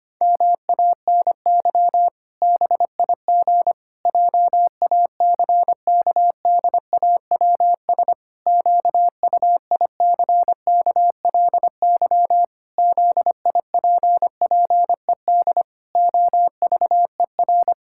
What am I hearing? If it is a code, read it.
Morse code, heard as a keyed tone, about 25 wpm, 700 Hz. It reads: MANY BIG JACKDAWS QUICKLY ZIPPED OVER
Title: Pangram Challenge v2 - 25wpm